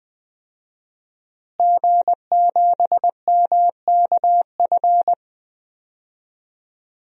Morse code, heard as a keyed tone, about 20 wpm, 700 Hz. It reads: G7MKF
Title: DXCC-100 Call Signs - 5 Characters - Repeated 2x - 20wpm